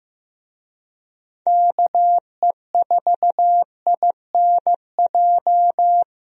Morse code, heard as a keyed tone, about 15 wpm, 700 Hz. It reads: KE4INJ
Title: USA Call Signs - 6 Characters - Repeated 3x - 15wpm